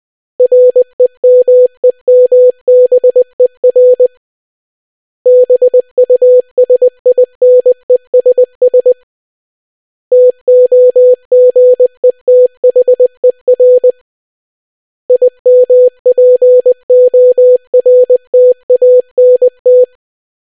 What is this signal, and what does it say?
Morse code, heard as a keyed tone, about 20 wpm, 500 Hz. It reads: REMEMBER BUSINESS TOGETHER IMPORTANT